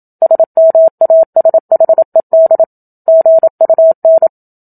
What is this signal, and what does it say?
Morse code, heard as a keyed tone, about 27 wpm, 650 Hz. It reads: SMASHED GUN
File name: Book Two / 01-Chapter 1 - 27 WPM